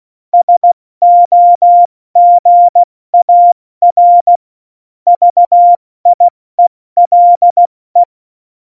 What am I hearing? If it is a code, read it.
Morse code, heard as a keyed tone, about 16 wpm, 700 Hz. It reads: SOGAR VIELE
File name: Morsekurs / Witze / Witz 18. Dezember, 70 F80